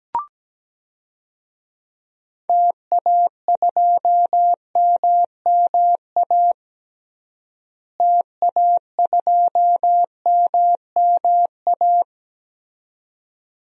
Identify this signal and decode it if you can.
Morse code, heard as a keyed tone, about 17 wpm, 700 Hz. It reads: TA2MMA TA2MMA
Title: DXCC-100 Call Signs - 6 Characters - Repeated 2x - 17wpm